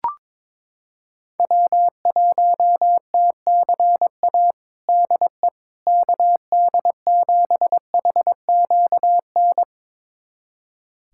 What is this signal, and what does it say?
Morse code, heard as a keyed tone, about 22 wpm, 700 Hz. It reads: W1TCA DE KD7HQN
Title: USA Callsign DE Callsign 22wpm